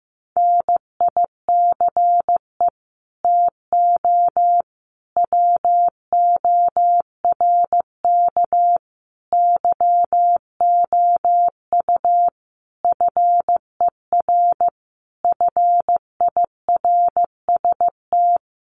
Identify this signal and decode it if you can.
Morse code, heard as a keyed tone, about 15 wpm, 700 Hz. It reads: NICE TO WORK YOU FER FIRST
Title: Common QSO Phrases 15wpm